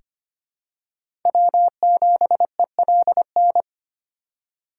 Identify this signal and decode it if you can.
Morse code, heard as a keyed tone, about 25 wpm, 700 Hz. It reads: W7ELN
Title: USA Call Signs - 4x Character Spacing - 25wpm